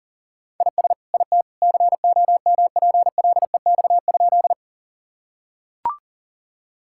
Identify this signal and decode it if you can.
Morse code, heard as a keyed tone, about 37 wpm, 700 Hz. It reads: IS IT COMPLEX?
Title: Sentences from Top 1000 Words - Review Part 6 - 40wpm